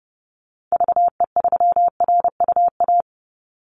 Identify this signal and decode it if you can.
Morse code, heard as a keyed tone, about 30 wpm, 700 Hz. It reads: VE3RUA
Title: Call Signs - 30wpm